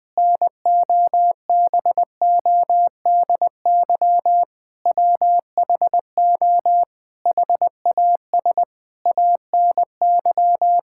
Morse code, heard as a keyed tone, about 20 wpm, 700 Hz. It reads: NOBODY WHO HAS ANY
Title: Sentences from Top 1600 Words 20wpm